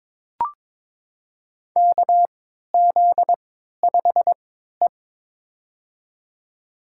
Morse code, heard as a keyed tone, about 22 wpm, spaced as for 12 wpm, 700 Hz. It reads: KZ5E
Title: USA Call Signs - 4 Characters - 3x Character Spacing - 22wpm